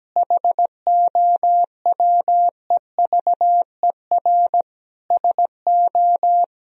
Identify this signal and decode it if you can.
Morse code, heard as a keyed tone, about 17 wpm, 700 Hz. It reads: HOWEVER SO
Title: Sentences from Top 1000 Words - Review Part 2 - 17wpm